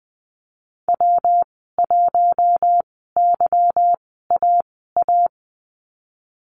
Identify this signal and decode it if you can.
Morse code, heard as a keyed tone, about 20 wpm, spaced as for 15 wpm, 700 Hz. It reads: W1YAA
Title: 147 Lesson 24g - Letter y - Call Signs 20wpm